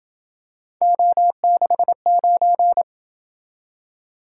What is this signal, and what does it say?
Morse code, heard as a keyed tone, about 26 wpm, 700 Hz. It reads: O69